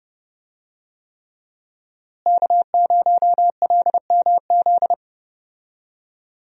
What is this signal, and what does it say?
Morse code, heard as a keyed tone, about 30 wpm, 700 Hz. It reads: K0LMZ
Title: USA Call Signs - 5 Characters - Repeated 3x - 30wpm